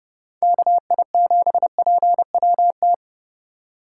Morse code, heard as a keyed tone, about 38 wpm, 700 Hz. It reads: KI7PWT